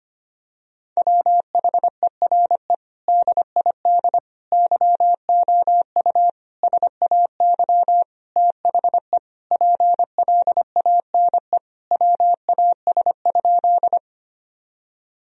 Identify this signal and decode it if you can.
Morse code, heard as a keyed tone, about 25 wpm, 700 Hz. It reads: WHERE DID YOU SAY THE PLANE WAS?
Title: Sentences from Top 1000 Words - Review Part 3 - 25wpm